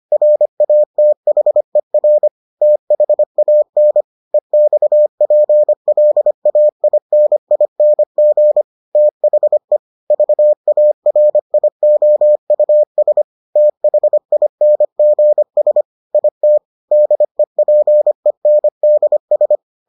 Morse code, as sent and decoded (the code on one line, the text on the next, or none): .-. .- - .... .-. / - .... .- -. / . -..- .--. .-.. .- .. -. .. -. --. / - .... / ...- .- .-. .. --- ..- ... / - .... .. -. --. ... / .. - / -.. .--. -. -.. ...
RATHER THAN EXPLAINING THE VARIOUS THINGS IT DEPENDS